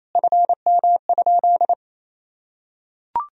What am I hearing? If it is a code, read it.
Morse code, heard as a keyed tone, about 27 wpm, 700 Hz. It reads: FM?
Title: Sentences from Top 300 Words and Abbreviations 28wpm